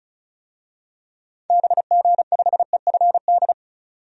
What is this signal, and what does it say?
Morse code, heard as a keyed tone, about 36 wpm, 700 Hz. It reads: BG5EFD
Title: DXCC-100 Call Signs - 6 Characters - 35wpm